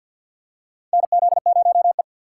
Morse code, heard as a keyed tone, about 46 wpm, 700 Hz. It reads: NZ0E
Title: USA Call Signs - 4 Characters - 2x Character Spacing - 50wpm